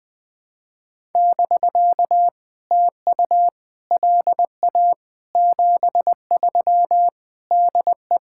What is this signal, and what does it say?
Morse code, heard as a keyed tone, about 20 wpm, 690 Hz. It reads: <BK> TU LA 73 DE
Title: POTA QSO Practice - 20wpm